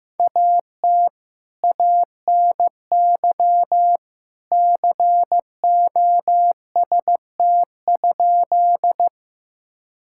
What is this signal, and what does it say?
Morse code, heard as a keyed tone, about 15 wpm, 700 Hz. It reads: AT ANY COST?
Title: Sentences from Top 1000 Words - Review Part 5 - 15wpm